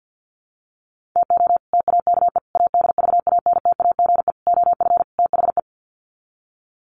Morse code, heard as a keyed tone, about 50 wpm, 700 Hz. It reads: TO TAKE ADVANTAGE OF THE
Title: Sets of 5 Words 50wpm